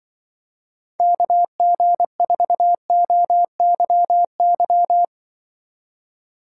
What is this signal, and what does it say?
Morse code, heard as a keyed tone, about 25 wpm, 700 Hz. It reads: KG4OYY
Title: Speed Racing - USA 2x3 Call Signs 20wpm